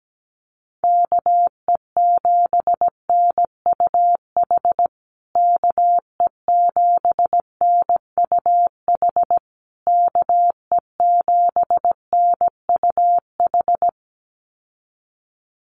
Morse code, heard as a keyed tone, about 17 wpm, 700 Hz. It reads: KE7NUH KE7NUH KE7NUH
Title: USA Call Signs Repeated 3x 17wpm